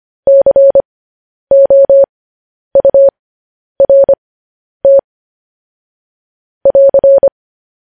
Morse code, heard as a keyed tone, about 25 wpm, spaced as for 10 wpm, 550 Hz. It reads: COURT <AR>
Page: News Headlines in Morse Code at 10 WPM